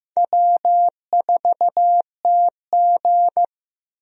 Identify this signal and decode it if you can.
Morse code, heard as a keyed tone, about 16 wpm, 700 Hz. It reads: W4TG
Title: USA Call Signs - 4 Characters - Repeated 3x - 15wpm